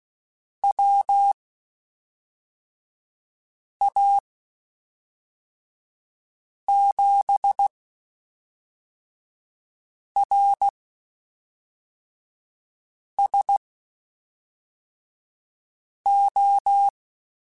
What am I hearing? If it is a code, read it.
Morse code, heard as a keyed tone, about 16 wpm, spaced as for 3 wpm, 800 Hz. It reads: WA7RSO
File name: (spaced) / QSOs Spaced-mp3s / WD0FBT 1 @5 sp